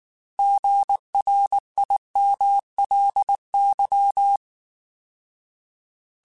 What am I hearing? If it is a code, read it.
Morse code, heard as a keyed tone, about 19 wpm, 800 Hz. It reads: GRIMLY